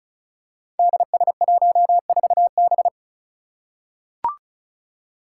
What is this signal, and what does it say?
Morse code, heard as a keyed tone, about 36 wpm, 700 Hz. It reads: DS14B
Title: Five Challenging Letter-Number Combinations 35wpm